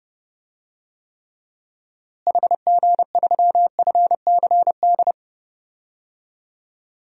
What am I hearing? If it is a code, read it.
Morse code, heard as a keyed tone, about 29 wpm, 700 Hz. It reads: HG3FCD
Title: DXCC-100 Call Signs - 6 Characters - Repeated 2x - 30wpm